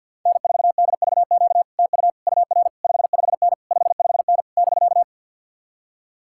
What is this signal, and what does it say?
Morse code, heard as a keyed tone, about 67 wpm, spaced as for 58 wpm, 700 Hz. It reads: N4DVQ TU UR 55N 55N <BK>
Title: Speed-Racing Common QSO Phrases - 50wpm